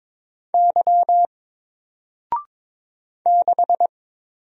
Morse code, heard as a keyed tone, about 22 wpm, 700 Hz. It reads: Y 6
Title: Single Character (Letter:Number:Punctuation:Prosign) Alt 22wpm